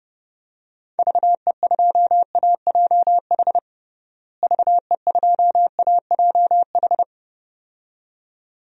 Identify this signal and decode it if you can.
Morse code, heard as a keyed tone, about 30 wpm, 700 Hz. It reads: VE2AJH VE2AJH
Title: DXCC-100 Call Signs - 6 Characters - Repeated 2x - 30wpm